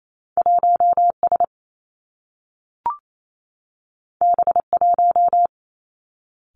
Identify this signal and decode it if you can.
Morse code, heard as a keyed tone, about 28 wpm, 700 Hz. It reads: 1 S B 1
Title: Two Challenging Letter-Number Combinations 28wpm